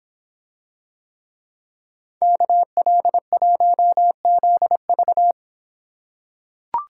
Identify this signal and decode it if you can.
Morse code, heard as a keyed tone, about 25 wpm, 700 Hz. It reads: KL1ZV